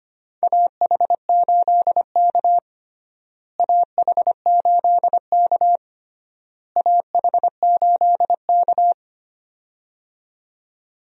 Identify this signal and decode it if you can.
Morse code, heard as a keyed tone, about 25 wpm, 700 Hz. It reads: AH8K AH8K AH8K
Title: USA Call Signs - 4 Characters - Repeated 3x - 25wpm